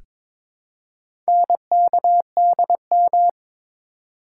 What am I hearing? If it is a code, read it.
Morse code, heard as a keyed tone, about 23 wpm, 700 Hz. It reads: NKDM
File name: practice / 22wpm / Four Letter-Number Combinations - 22wpm